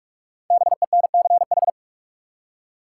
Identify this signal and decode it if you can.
Morse code, heard as a keyed tone, about 46 wpm, 700 Hz. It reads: BENCH
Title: Five Letter Words - 45wpm